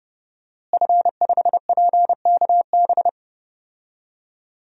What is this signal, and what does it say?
Morse code, heard as a keyed tone, about 30 wpm, 700 Hz. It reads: F5PKB
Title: DXCC-100 Call Signs - 5 Characters - Repeated 3x - 30wpm